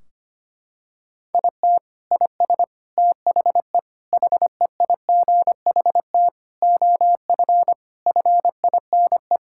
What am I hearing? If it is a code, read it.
Morse code, heard as a keyed tone, about 25 wpm, 700 Hz. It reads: IT IS THE HEIGHT OF FINE
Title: Sentences from Top 1600 Words 25wpm